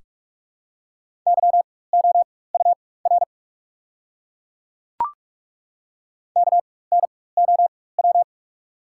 Morse code, heard as a keyed tone, about 42 wpm, spaced as for 21 wpm, 690 Hz. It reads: YOUR KNOW
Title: 4x Character Spacing - Top 200 Words - 45wpm